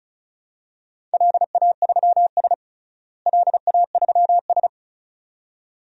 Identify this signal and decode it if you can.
Morse code, heard as a keyed tone, about 36 wpm, 700 Hz. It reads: LA3S LA3S
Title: DXCC-100 Call Signs - 4 Characters - Repeated 2x - 35wpm